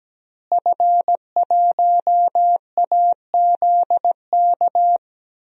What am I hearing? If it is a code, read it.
Morse code, heard as a keyed tone, about 17 wpm, 700 Hz. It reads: F1AZK
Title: DXCC-100 Call Signs - 5 Characters - Repeated 2x - 17wpm